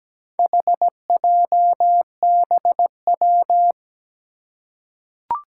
Morse code, heard as a keyed tone, about 17 wpm, 700 Hz. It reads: HJBW
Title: Four Challenging Letter-Number Combinations 17wpm